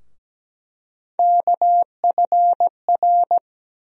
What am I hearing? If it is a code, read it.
Morse code, heard as a keyed tone, about 17 wpm, 700 Hz. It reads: KFR
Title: USA Call Sign Postfixes - 3 Letters 17wpm